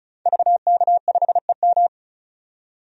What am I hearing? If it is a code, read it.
Morse code, heard as a keyed tone, about 36 wpm, 700 Hz. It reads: VK5EM
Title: DXCC-100 Call Signs - 5 Characters - 35wpm